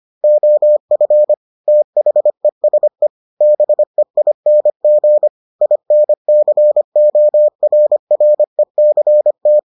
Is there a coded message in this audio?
Yes: Morse code from 0.2 to 9.6 s, heard as a keyed tone, about 25 wpm, 600 Hz. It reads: OF THESE BEING INCORRECT